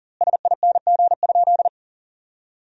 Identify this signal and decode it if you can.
Morse code, heard as a keyed tone, about 40 wpm, 700 Hz. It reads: SING?
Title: 2x Word-Spacing - Sentences from Top 400 Words - 40wpm